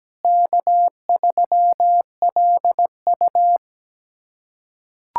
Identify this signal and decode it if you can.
Morse code, heard as a keyed tone, about 17 wpm, 700 Hz. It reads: K3LU
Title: K1USN SST QSO Practice - 17wpm